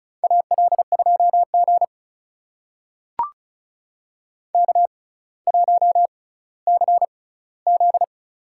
Morse code, heard as a keyed tone, about 36 wpm, spaced as for 18 wpm, 700 Hz. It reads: AL2G K1CZ
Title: USA Call Signs - 4 Characters - 6x Character Spacing - 35wpm